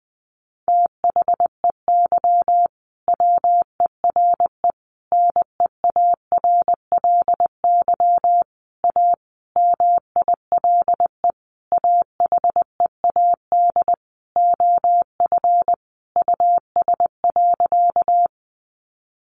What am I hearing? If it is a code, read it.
Morse code, heard as a keyed tone, about 20 wpm, 700 Hz. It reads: THEY WERE NEARLY A MILE AHEAD OF US.
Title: Sentences from Top 1300 Words 20wpm